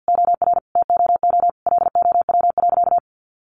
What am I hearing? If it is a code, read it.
Morse code, heard as a keyed tone, about 50 wpm, 700 Hz. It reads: OR TOO LOW.
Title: Sentences from Top 1200 Words 50wpm